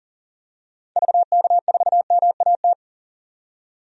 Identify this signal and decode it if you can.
Morse code, heard as a keyed tone, about 40 wpm, 700 Hz. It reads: VK4MAT
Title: DXCC-100 Call Signs - 6 Characters - 40wpm